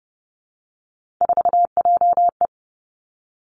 Morse code, heard as a keyed tone, about 30 wpm, 700 Hz. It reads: 4JE
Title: Three Challenging Letter-Number Combinations 30wpm